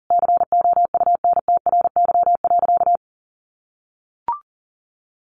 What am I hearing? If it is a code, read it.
Morse code, heard as a keyed tone, about 40 wpm, 700 Hz. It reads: COUNTRY.
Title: Sentences from Top 300 Words 40wpm